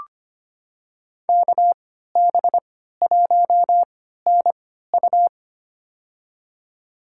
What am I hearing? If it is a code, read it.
Morse code, heard as a keyed tone, about 25 wpm, spaced as for 14 wpm, 700 Hz. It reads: KB1NU
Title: USA Call Signs - 3x Character Spacing - 25wpm